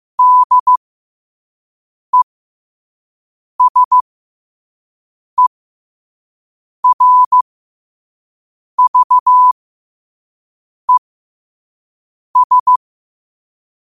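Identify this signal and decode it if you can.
Morse code, heard as a keyed tone, about 15 wpm, spaced as for 5 wpm, 1000 Hz. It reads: DESERVES
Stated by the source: Morse practice at 5 wpm.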